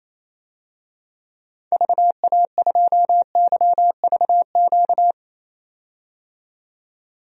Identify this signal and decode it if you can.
Morse code, heard as a keyed tone, about 28 wpm, 700 Hz. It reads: VA2YVQ
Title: DXCC-100 Call Signs - 6 Characters - 28wpm